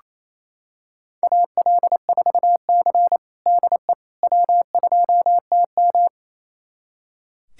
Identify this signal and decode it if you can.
Morse code, heard as a keyed tone, about 28 wpm, 700 Hz. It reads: AL4C DE W2TM
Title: USA Callsign DE Callsign - 4 Characters - 28wpm